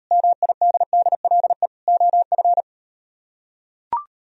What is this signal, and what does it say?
Morse code, heard as a keyed tone, about 38 wpm, 700 Hz. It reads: MIDDLE OF